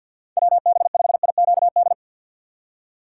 Morse code, heard as a keyed tone, about 48 wpm, 690 Hz. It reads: WB5IQD
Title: USA Call Signs - 6 Characters - 8x Character Spacing - 50wpm